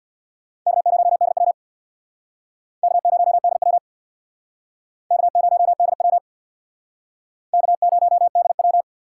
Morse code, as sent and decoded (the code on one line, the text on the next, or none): -.- ----- -.. .-- / -.- ----- -.. .-- / -.- ----- -.. .-- / -.- ----- -.. .--
K0DW K0DW K0DW K0DW